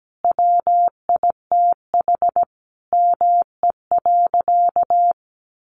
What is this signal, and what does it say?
Morse code, heard as a keyed tone, about 17 wpm, 700 Hz. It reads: WITH ME.
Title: Sentences from Top 800 Words 17wpm